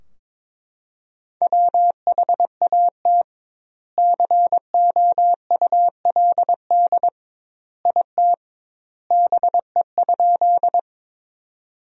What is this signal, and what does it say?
Morse code, heard as a keyed tone, about 23 wpm, 700 Hz. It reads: WHAT COULD IT BE?
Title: 2x Word-Spacing - Sentences from Top 100 Words - 22wpm